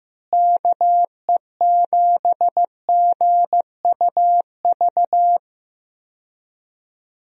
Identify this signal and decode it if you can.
Morse code, heard as a keyed tone, about 15 wpm, 700 Hz. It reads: KE7GUV
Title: USA Call Signs - 6 Characters - Repeated 2x - 15wpm